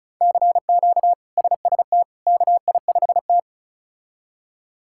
Morse code, heard as a keyed tone, about 36 wpm, 700 Hz. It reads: CQ SST KI5T
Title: K1USN SST QSO Practice - 35wpm